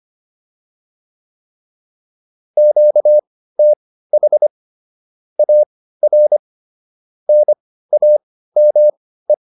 Morse code, heard as a keyed tone, about 25 wpm, spaced as for 15 wpm, 600 Hz. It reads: QTH AR NAME